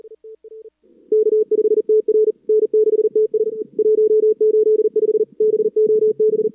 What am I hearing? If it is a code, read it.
Morse code, heard as a keyed tone, about 39 wpm, 430 Hz. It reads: K5TR N6TS 185 BOB
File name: CWSprint / Feb2022 / k5tr-feb-na-sprint-cw-k5ot -0100